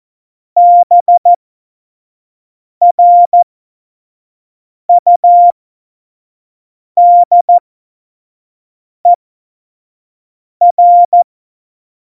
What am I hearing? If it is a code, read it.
Morse code, heard as a keyed tone, about 14 wpm, spaced as for 5 wpm, 700 Hz. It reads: BRUDER
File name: Morsekurs / Witze / Witz 03. November 2020, 25 F70